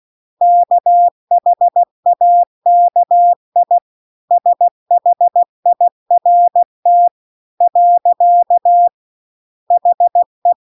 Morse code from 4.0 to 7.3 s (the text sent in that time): SHIRT